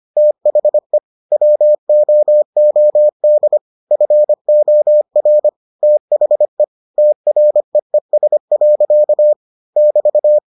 Morse code, heard as a keyed tone, about 25 wpm, 600 Hz. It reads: THE WOOD FOR THE TREES. <BT>